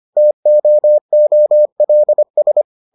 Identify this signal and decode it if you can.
Morse code, heard as a keyed tone, about 25 wpm, 600 Hz. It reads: TOOLS